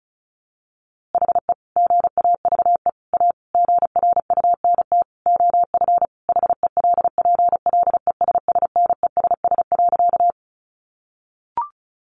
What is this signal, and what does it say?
Morse code, heard as a keyed tone, about 34 wpm, 700 Hz. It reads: HE GAVE A GRUNT OF HELPLESSNESS.
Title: A Sound Of Thunder 35wpm